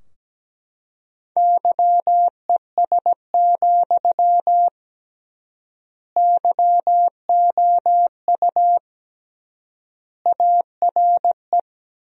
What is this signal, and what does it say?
Morse code, heard as a keyed tone, about 17 wpm, 700 Hz. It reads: YES, YOU ARE
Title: 3x Word-Spacing - Sentences from Top 500 Words - 17wpm